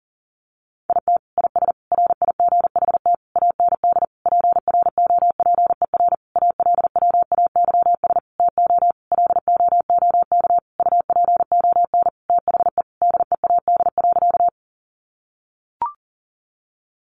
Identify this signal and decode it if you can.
Morse code, heard as a keyed tone, about 40 wpm, 700 Hz. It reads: IT IS RIGHT AND PROPER ALWAYS TO LOOK UPON THE DEAD.
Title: Sentences from Top 1500 Words 40wpm